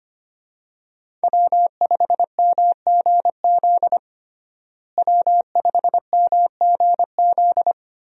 Morse code, heard as a keyed tone, about 25 wpm, 700 Hz. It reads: W5MGZ W5MGZ
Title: USA Call Signs - 5 Characters - Repeated 2x - 25wpm